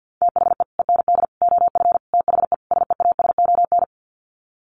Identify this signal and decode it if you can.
Morse code, heard as a keyed tone, about 50 wpm, 700 Hz. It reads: THE END OF THE SEASON
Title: Sets of 5 Words Encore 50wpm